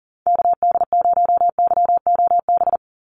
Morse code, heard as a keyed tone, about 40 wpm, 700 Hz. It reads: KD0YOB
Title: NA SOTA Activator Callsigns 40wpm